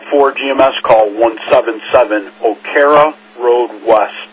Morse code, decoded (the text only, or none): none